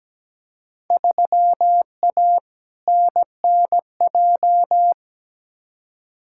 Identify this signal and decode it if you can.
Morse code, heard as a keyed tone, about 17 wpm, 700 Hz. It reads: 3A NNJ
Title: Field Day Exchanges Encore - 17wpm